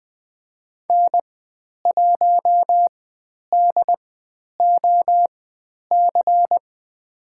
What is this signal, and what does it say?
Morse code, heard as a keyed tone, about 20 wpm, spaced as for 10 wpm, 700 Hz. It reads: N1DOC
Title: 109 Lesson 18g - Letter m - Call Signs 20wpm